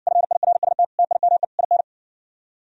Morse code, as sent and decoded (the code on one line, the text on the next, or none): ..-. .. .-. ... - / - .. -- . / .. -.
FIRST TIME IN